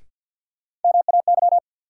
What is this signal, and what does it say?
Morse code, heard as a keyed tone, about 53 wpm, 700 Hz. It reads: MAY
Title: Sets of 2 Words - New Word Drill - 50wpm